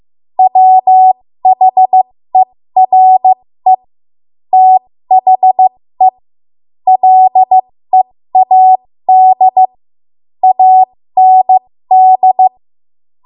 Morse code, heard as a keyed tone, about 15 wpm, spaced as for 13 wpm, 750 Hz. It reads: WHERE THE LEAD AND
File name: Morse / 13WPM